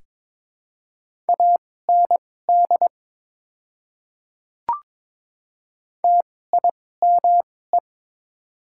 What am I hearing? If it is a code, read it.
Morse code, heard as a keyed tone, about 21 wpm, spaced as for 16 wpm, 700 Hz. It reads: AND TIME
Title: 2x Character Spacing - Top 100 Words - 22wpm